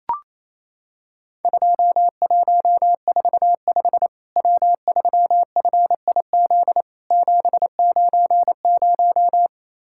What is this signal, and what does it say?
Morse code, heard as a keyed tone, about 28 wpm, 700 Hz. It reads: 2145 W3FIZ 790
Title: Worked All Europe DX Contest - QTC Exchange - 28wpm